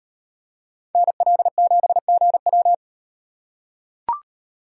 Morse code, heard as a keyed tone, about 37 wpm, 700 Hz. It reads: NL7GW